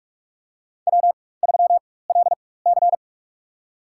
Morse code, heard as a keyed tone, about 46 wpm, spaced as for 21 wpm, 700 Hz. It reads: W3LC
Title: USA Call Signs - 4x Character Spacing - 45wpm